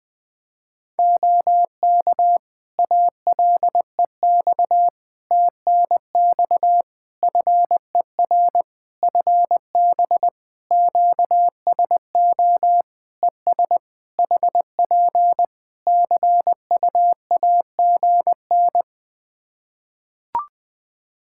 Morse code, heard as a keyed tone, about 20 wpm, 700 Hz. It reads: OK ALEX TNX FER FB QSO ES HP CUAGN
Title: Practice QSO - Exchange 4 - Ending - 20wpm